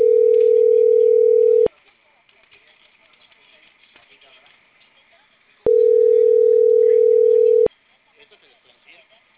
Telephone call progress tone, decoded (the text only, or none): ringback tone